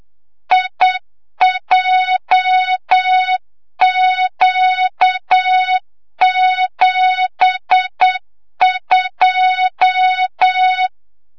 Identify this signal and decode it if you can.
Morse code, heard as a keyed tone, about 8 wpm, 3700 Hz, with noise. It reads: IJQ72
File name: Morze-HA2MN / hang / 25 GYAKORLAT